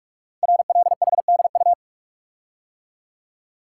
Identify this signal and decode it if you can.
Morse code, heard as a keyed tone, about 44 wpm, 700 Hz. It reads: RLHDU
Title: Code Groups - 5 Letters - 45wpm